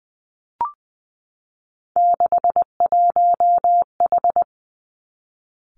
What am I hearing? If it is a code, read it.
Morse code, heard as a keyed tone, about 20 wpm, 700 Hz. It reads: 61H
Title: Three Challenging Letter-Number Combinations 20wpm